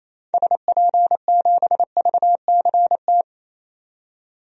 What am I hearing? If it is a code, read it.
Morse code, heard as a keyed tone, about 28 wpm, 700 Hz. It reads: SP7VCT
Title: DXCC-100 Call Signs - 6 Characters - 28wpm